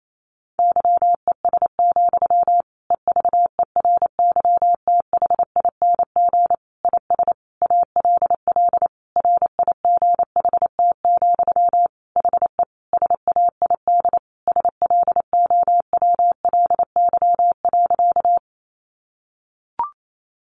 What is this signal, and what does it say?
Morse code, heard as a keyed tone, about 28 wpm, 700 Hz. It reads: YES, EVERYTHING IS ALL RIGHT, HE SAID SLOWLY.
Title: Sentences from Top 1100 Words 28wpm